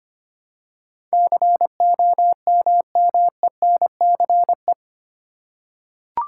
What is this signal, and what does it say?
Morse code, heard as a keyed tone, about 25 wpm, 700 Hz. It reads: COMMENCE
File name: 8 Letter Words - First 3 Letters Given - 25wpm